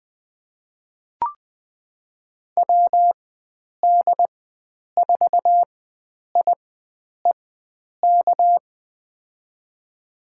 Morse code, heard as a keyed tone, about 20 wpm, spaced as for 9 wpm, 700 Hz. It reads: WD4IEK
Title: USA Call Signs - 4x Character Spacing - 20wpm